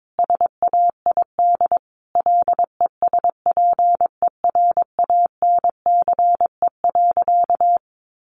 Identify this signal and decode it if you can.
Morse code, heard as a keyed tone, about 22 wpm, 700 Hz. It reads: SAID LESPERANCE.
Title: A Sound Of Thunder 22wpm